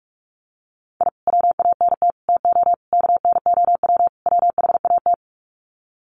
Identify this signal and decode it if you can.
Morse code, heard as a keyed tone, about 44 wpm, 700 Hz. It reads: I WANT TO KNOW WHAT
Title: Sets of 5 Words Encore2 45wpm